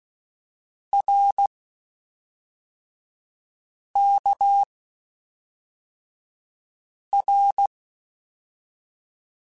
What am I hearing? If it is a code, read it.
Morse code, heard as a keyed tone, about 16 wpm, 800 Hz. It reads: R K R
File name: Basic Sets / Basic Letter Groups Collective7 / RKPX@5